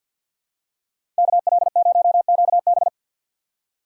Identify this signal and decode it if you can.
Morse code, heard as a keyed tone, about 52 wpm, 700 Hz. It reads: KL0QB
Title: USA Call Signs - 2x Character Spacing - 50wpm